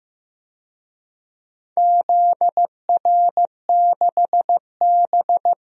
Morse code, heard as a keyed tone, about 15 wpm, 700 Hz. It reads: ZR6B
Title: DXCC-100 Call Signs - 4 Characters - 15wpm